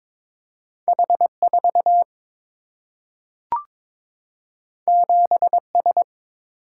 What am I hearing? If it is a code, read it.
Morse code, heard as a keyed tone, about 22 wpm, 700 Hz. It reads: H4 7S
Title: Single-Letter Single-Number 22wpm